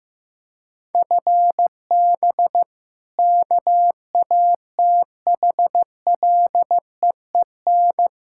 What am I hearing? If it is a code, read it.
Morse code, heard as a keyed tone, about 16 wpm, 700 Hz. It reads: FB KATHLEEN
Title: Practice QSO - Exchange 3 - More Info - 15wpm